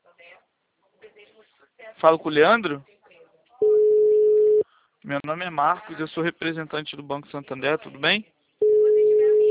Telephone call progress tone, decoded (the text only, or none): ringback tone